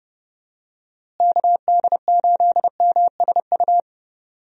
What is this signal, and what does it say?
Morse code, heard as a keyed tone, about 32 wpm, 700 Hz. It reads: KD8MSU
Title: USA Call Signs - 6 Characters - Repeated 3x - 30wpm